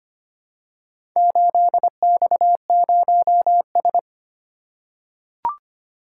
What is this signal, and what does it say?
Morse code, heard as a keyed tone, about 25 wpm, 700 Hz. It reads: 8X0S
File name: practice / 25wpm / Four Letter-Number Combinations - 25wpm